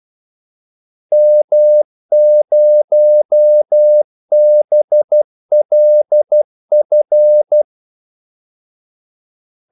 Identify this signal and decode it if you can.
Morse code, heard as a keyed tone, about 12 wpm, 600 Hz. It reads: M0BLF
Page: 2 ekstra ordmellemrum mellem kaldesignalerne